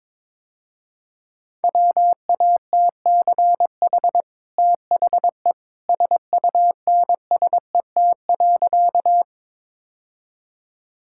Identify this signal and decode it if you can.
Morse code, heard as a keyed tone, about 22 wpm, 700 Hz. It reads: WATCH THE SUNSET.
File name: practice / 22wpm / Easy Sentences with Hints - 22wpm